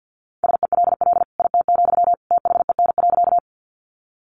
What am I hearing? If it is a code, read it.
Morse code, heard as a keyed tone, about 50 wpm, 700 Hz. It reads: SELL IT, THEN.
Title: Sentences from Top 1100 Words 50wpm